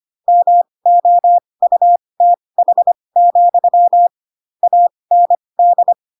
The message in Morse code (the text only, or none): MOUTH, AND